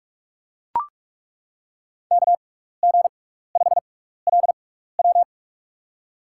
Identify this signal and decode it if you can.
Morse code, heard as a keyed tone, about 41 wpm, spaced as for 15 wpm, 700 Hz. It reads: KG5LW
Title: USA Call Signs - 6x Character Spacing - 45wpm